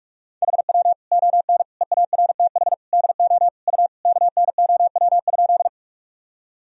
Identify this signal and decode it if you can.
Morse code, heard as a keyed tone, about 47 wpm, 700 Hz. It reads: HW ON EARTH DO U KNOW?
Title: Sentences from Top 200 Words and Abbreviations 45wpm